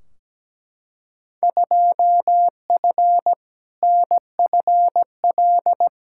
Morse code, heard as a keyed tone, about 17 wpm, 700 Hz. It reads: 2F NFL
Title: Field Day Exchanges - 17wpm